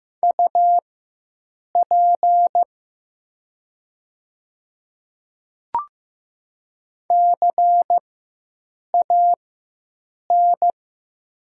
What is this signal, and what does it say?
Morse code, heard as a keyed tone, about 15 wpm, spaced as for 7 wpm, 700 Hz. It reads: UP CAN
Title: 4x Character Spacing - Top 100 Words - 15wpm